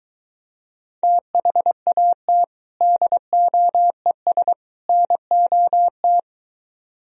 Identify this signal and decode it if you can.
Morse code, heard as a keyed tone, about 23 wpm, 700 Hz. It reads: THAT DOES NOT